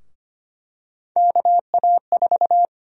Morse code, heard as a keyed tone, about 25 wpm, 700 Hz. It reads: KA4
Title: USA Call Sign Prefixes - Repeated 2x - 25wpm